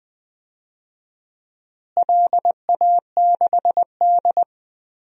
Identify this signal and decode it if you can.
Morse code, heard as a keyed tone, about 21 wpm, 700 Hz. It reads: LA6D
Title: DXCC-100 Call Signs - 4 Characters - Repeated 3x - 20wpm